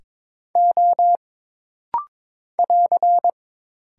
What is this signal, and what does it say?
Morse code, heard as a keyed tone, about 22 wpm, 700 Hz. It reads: O <AR>
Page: This rapid-fire timing is geared to push your ICR (Instant Character Recognition) skills.